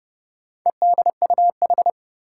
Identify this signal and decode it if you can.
Morse code, heard as a keyed tone, about 32 wpm, 700 Hz. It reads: EDUH
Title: Four Challenging Letter-Number Combinations 30wpm